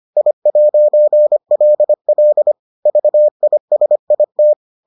Morse code, heard as a keyed tone, about 25 wpm, 600 Hz. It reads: I'LL VISIT